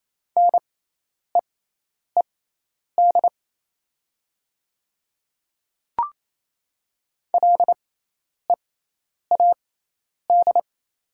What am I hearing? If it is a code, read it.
Morse code, heard as a keyed tone, about 28 wpm, spaced as for 10 wpm, 700 Hz. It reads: NEED LEAD
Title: 6x Character Spacing - Top 500 Words - 28wpm